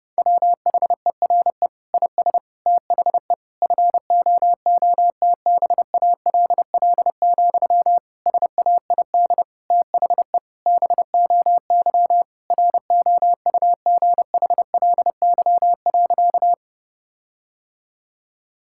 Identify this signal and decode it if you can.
Morse code, heard as a keyed tone, about 30 wpm, 700 Hz. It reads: WHERE IS THE FOOTBALL, SAID THE BOY ROUGHLY.
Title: Sentences from Top 1600 Words 30wpm